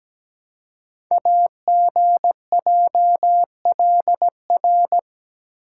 Morse code, heard as a keyed tone, about 17 wpm, 700 Hz. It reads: AGJLR
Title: Code Groups - 5 Letters - 17wpm